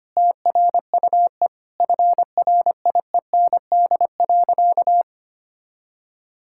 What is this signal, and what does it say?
Morse code, heard as a keyed tone, about 25 wpm, 700 Hz. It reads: TRUE FRIEND.
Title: Sentences from Top 1000 Words - Review Part 6 - 25wpm